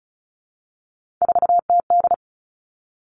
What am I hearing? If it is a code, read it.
Morse code, heard as a keyed tone, about 36 wpm, 700 Hz. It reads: 4TD
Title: Three Challenging Letter-Number Combinations 35wpm